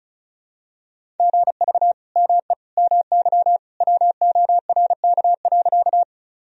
Morse code, heard as a keyed tone, about 34 wpm, 690 Hz. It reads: GV ME MY WORK.
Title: Sentences from Top 200 Words and Abbreviations 35wpm